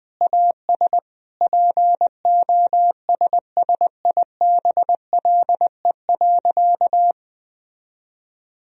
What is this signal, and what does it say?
Morse code, heard as a keyed tone, about 20 wpm, 700 Hz. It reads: AS POSSIBLE.
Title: Sentences from Top 1000 Words - Review Part 2 - 20wpm